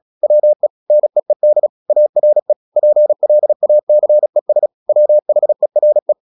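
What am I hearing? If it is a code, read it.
Morse code, heard as a keyed tone, about 36 wpm, 600 Hz. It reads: WE NEED ARE PLACES WHERE
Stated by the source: Morse practice at 36 wpm, text of Ken Follett's novel Never, from CW med Gnister.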